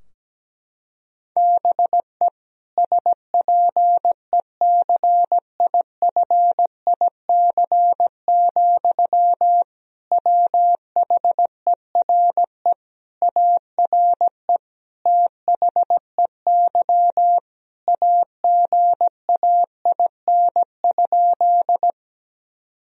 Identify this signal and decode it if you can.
Morse code, heard as a keyed tone, about 17 wpm, 700 Hz. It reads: BE SPECIFIC, WHERE ARE THEY AGAIN?
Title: Sentences from Top 1000 Words - Review Part 3 - 17wpm